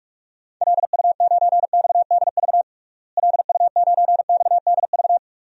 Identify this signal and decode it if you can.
Morse code, heard as a keyed tone, about 42 wpm, 700 Hz. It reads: LU9XDV LU9XDV
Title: DXCC-100 Call Signs - 6 Characters - Repeated 3x - 45wpm